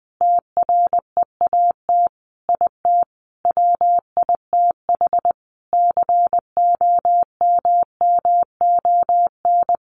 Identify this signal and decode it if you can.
Morse code, heard as a keyed tone, about 20 wpm, 700 Hz. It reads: TREAT IT WITH COMMON
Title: Sentences from Top 1300 Words 20wpm